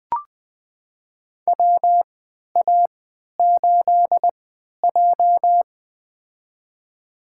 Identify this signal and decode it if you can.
Morse code, heard as a keyed tone, about 20 wpm, spaced as for 11 wpm, 700 Hz. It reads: WA8J
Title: USA Call Signs - 4 Characters - 3x Character Spacing - 20wpm